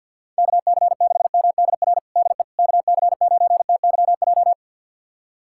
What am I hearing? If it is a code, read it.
Morse code, heard as a keyed tone, about 47 wpm, 700 Hz. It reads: KC6MDR DE KC9TYJ
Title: USA Callsign DE Callsign - 6 Characters - 50wpm